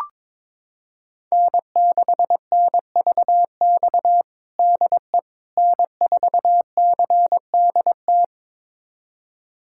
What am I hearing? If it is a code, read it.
Morse code, heard as a keyed tone, about 22 wpm, 700 Hz. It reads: N6NVX DE N4CDT
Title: USA Callsign DE Callsign - 5 Characters - 22wpm